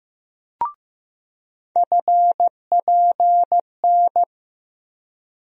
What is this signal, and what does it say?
Morse code, heard as a keyed tone, about 15 wpm, 700 Hz. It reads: FPN
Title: USA Call Sign Postfixes - 3 Letters 15wpm